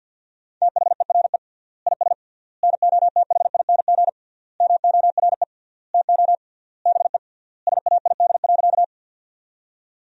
Morse code, heard as a keyed tone, about 50 wpm, 700 Hz. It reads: THERE IS NOTHING MORE TO BE SAID.
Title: 3x Word-Spacing - Sentences from Top 500 Words - 50wpm